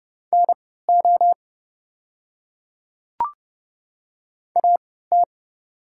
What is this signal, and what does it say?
Morse code, heard as a keyed tone, about 30 wpm, spaced as for 17 wpm, 690 Hz. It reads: NO AT
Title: 3x Character Spacing - Top 100 Words - 30wpm